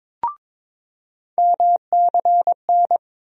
Morse code, heard as a keyed tone, about 22 wpm, 700 Hz. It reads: MCN
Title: USA Call Sign Postfixes - 3 Letters 22wpm